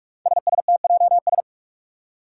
Morse code, heard as a keyed tone, about 45 wpm, 700 Hz. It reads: SSTJS
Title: Five Challenging Letter-Number Combinations 45wpm